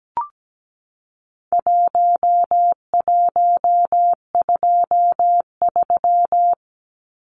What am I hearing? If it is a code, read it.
Morse code, heard as a keyed tone, about 17 wpm, 700 Hz. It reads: 1123
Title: Numbers - 4-Digit 17wpm